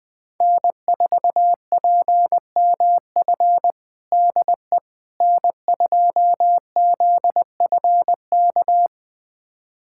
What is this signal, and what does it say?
Morse code, heard as a keyed tone, about 20 wpm, 700 Hz. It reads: N4PMF DE N2ZFK
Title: USA Callsign DE Callsign - 5 Characters - 20wpm